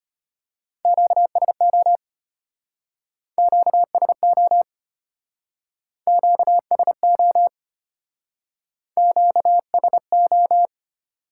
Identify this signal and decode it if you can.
Morse code, heard as a keyed tone, about 30 wpm, 700 Hz. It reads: QSO QSO QSO QSO
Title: Speed-Racing Top 180 QSO Elements 25wpm